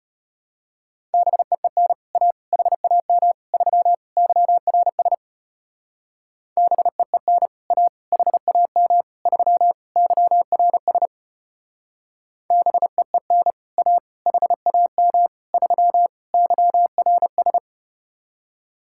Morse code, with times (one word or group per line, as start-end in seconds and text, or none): 1.1-1.9 BEEN
2.1-2.3 A
2.5-3.3 HAM
3.5-3.9 3
4.2-5.1 YRS
6.6-7.5 BEEN
7.7-7.9 A
8.1-9.0 HAM
9.3-9.7 3
10.0-11.1 YRS
12.5-13.5 BEEN
13.8-14.0 A
14.3-15.3 HAM
15.5-16.1 3
16.3-17.6 YRS